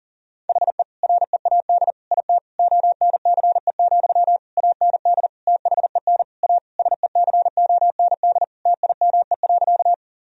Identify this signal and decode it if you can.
Morse code, heard as a keyed tone, about 37 wpm, 700 Hz. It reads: HE READ IT ONCE, AND THEN A SECOND TIME.
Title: Sentences from Top 1000 Words - Review Part 1 - 40wpm